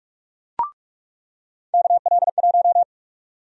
Morse code, heard as a keyed tone, about 42 wpm, 700 Hz. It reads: KL1
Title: USA Call Sign Prefixes - 45wpm